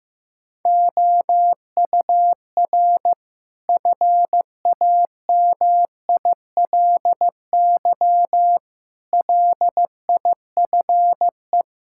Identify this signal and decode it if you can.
Morse code, heard as a keyed tone, about 15 wpm, 700 Hz. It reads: OUR FAMILY LIFE
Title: Sentences from Top 1000 Words - Review Part 5 - 15wpm